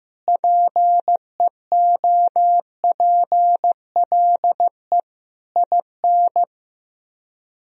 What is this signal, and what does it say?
Morse code, heard as a keyed tone, about 15 wpm, 700 Hz. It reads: PEOPLE IN
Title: Speed-Racing - Sets of 5 Words 15wpm